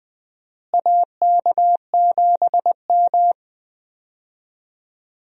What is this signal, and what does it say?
Morse code, heard as a keyed tone, about 20 wpm, 700 Hz. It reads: AK7M
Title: USA Call Signs - 4 Characters - Repeated 2x - 20wpm